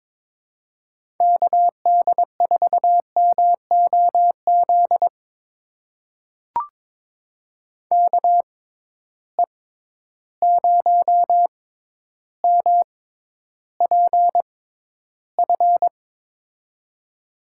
Morse code, heard as a keyed tone, about 22 wpm, spaced as for 11 wpm, 700 Hz. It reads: KD4MOZ KE0MPF
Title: USA Call Signs - 6 Characters - 6x Character Spacing - 22wpm